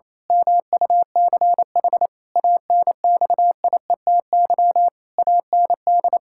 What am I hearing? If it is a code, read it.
Morse code, heard as a keyed tone, about 28 wpm, 700 Hz. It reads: MUCH ANXIETY AND
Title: Pangram Challenge v2 - 28wpm